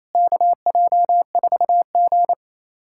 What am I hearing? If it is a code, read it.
Morse code, heard as a keyed tone, about 28 wpm, 700 Hz. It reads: KJ4G
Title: SOTA QSO Practice 28wpm